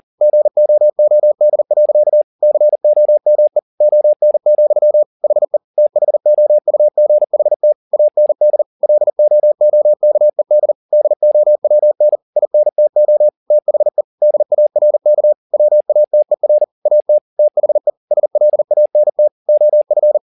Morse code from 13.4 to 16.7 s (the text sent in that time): THE DARK WATER